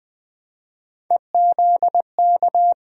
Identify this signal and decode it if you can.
Morse code, heard as a keyed tone, about 20 wpm, 700 Hz. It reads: EZK
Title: Three Letter-Number Combinations - 20wpm